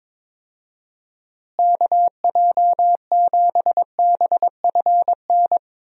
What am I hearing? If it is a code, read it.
Morse code, heard as a keyed tone, about 22 wpm, 700 Hz. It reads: KJ7BFN
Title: USA Call Signs - 6 Characters - Repeated 3x - 22wpm